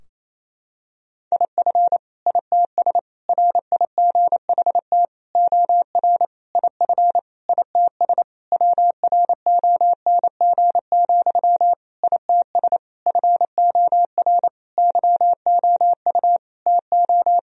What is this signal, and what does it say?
Morse code, heard as a keyed tone, about 28 wpm, 700 Hz. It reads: IF ITS RIGHT OR IF ITS WRONG, ITS FOR YOU TO
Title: Sentences from Top 1000 Words - Review Part 3 - 28wpm